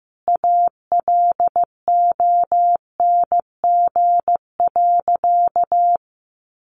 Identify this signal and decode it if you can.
Morse code, heard as a keyed tone, about 15 wpm, 700 Hz. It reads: ALONG.
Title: Sentences from Top 1500 Words 15wpm